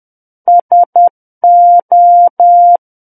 Morse code, heard as a keyed tone, about 10 wpm, 700 Hz. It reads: SO
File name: Book One / 01-Chapter 1 - 10 WPM